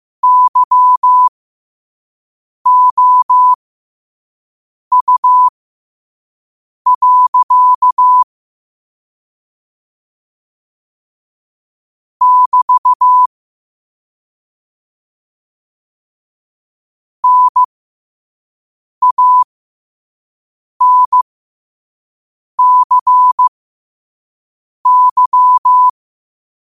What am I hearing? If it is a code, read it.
Morse code, heard as a keyed tone, about 15 wpm, spaced as for 5 wpm, 1000 Hz. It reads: YOU. <BT> NANCY